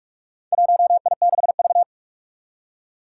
Morse code, heard as a keyed tone, about 46 wpm, 700 Hz. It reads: 1I6V
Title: Four Challenging Letter-Number Combinations 45wpm